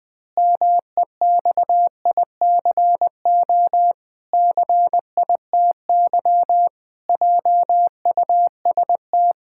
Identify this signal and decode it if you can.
Morse code, heard as a keyed tone, about 20 wpm, 700 Hz. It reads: MEXICO CITY JUST
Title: Pangram Challenge v2 - 20wpm